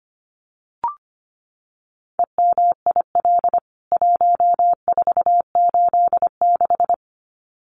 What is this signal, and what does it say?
Morse code, heard as a keyed tone, about 25 wpm, 700 Hz. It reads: EMIL 1486
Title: CWT Exchange 25wpm